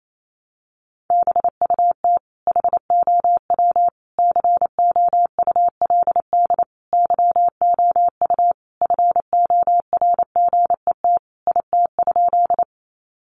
Morse code, heard as a keyed tone, about 28 wpm, 700 Hz. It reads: BUT HOW COULD YOU FORGET IT?
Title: Sentences from Top 1200 Words 28wpm